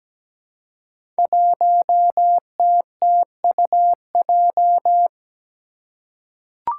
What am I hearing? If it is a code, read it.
Morse code, heard as a keyed tone, about 17 wpm, 700 Hz. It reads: 1TTUJ
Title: Five Challenging Letter-Number Combinations 17wpm